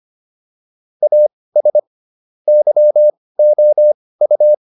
Morse code, heard as a keyed tone, about 25 wpm, spaced as for 18 wpm, 600 Hz. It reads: AS YOU